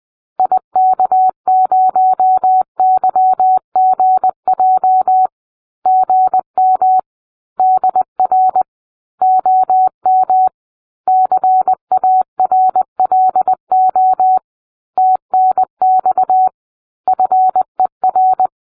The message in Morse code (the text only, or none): IK0YGJ GM DR OM CARLO TNX FER